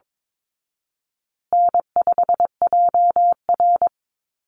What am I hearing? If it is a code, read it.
Morse code, heard as a keyed tone, about 22 wpm, 700 Hz. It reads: N5JR
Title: Call Signs Repeated 2x - 22wpm